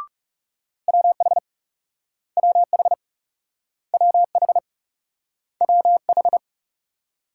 Morse code, heard as a keyed tone, about 38 wpm, 700 Hz. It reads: WH WH WH WH